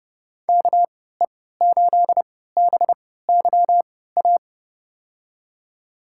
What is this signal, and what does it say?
Morse code, heard as a keyed tone, about 29 wpm, spaced as for 17 wpm, 700 Hz. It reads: KE8BYA
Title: USA Call Signs - 6 Characters - 3x Character Spacing - 30wpm